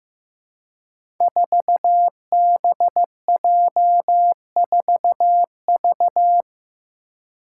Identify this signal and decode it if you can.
Morse code, heard as a keyed tone, about 15 wpm, 700 Hz. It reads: 4BJ4V
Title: Five Challenging Letter-Number Combinations 15wpm